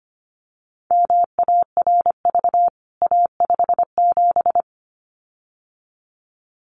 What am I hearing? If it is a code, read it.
Morse code, heard as a keyed tone, about 25 wpm, 700 Hz. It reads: MARV A57
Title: CWT Exchange 25wpm